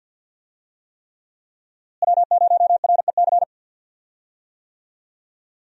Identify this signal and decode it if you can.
Morse code, heard as a keyed tone, about 52 wpm, 700 Hz. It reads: W0REC
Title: USA Call Signs - 5 Characters - Repeated 3x - 50wpm